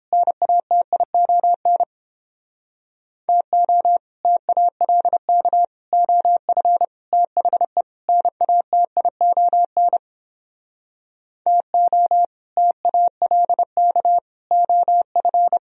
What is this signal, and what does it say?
Morse code, heard as a keyed tone, about 29 wpm, 700 Hz. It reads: NATION TO TALK OF THE NATION TO TALK OF